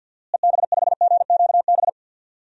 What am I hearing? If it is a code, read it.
Morse code, heard as a keyed tone, about 46 wpm, 700 Hz. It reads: EB5GQB
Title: DXCC-100 Call Signs - 6 Characters - Repeated 3x - 50wpm